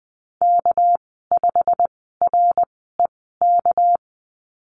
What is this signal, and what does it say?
Morse code, heard as a keyed tone, about 20 wpm, spaced as for 15 wpm, 700 Hz. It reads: K5REK
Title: 245 Lesson 39g - Letter z - Call Signs 20wpm